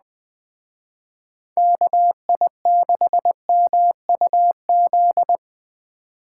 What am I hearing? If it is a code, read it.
Morse code, heard as a keyed tone, about 20 wpm, 700 Hz. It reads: KI6MUZ
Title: Speed Racing - USA Call Signs 17wpm